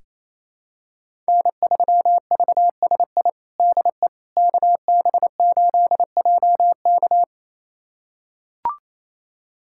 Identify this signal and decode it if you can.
Morse code, heard as a keyed tone, about 28 wpm, 700 Hz. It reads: N3VSI DE KB8JK
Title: USA Callsign DE Callsign - 5 Characters - 28wpm